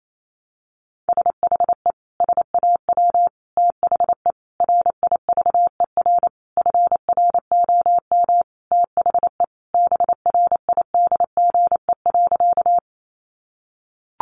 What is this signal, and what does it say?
Morse code, heard as a keyed tone, about 28 wpm, 700 Hz. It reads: SHE SAW THE RIVER FROM THE BRIDGE.
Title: Sentences from Top 1200 Words 28wpm